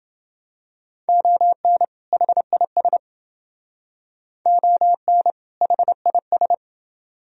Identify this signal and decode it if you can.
Morse code, heard as a keyed tone, about 28 wpm, 700 Hz. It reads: ON HIS ON HIS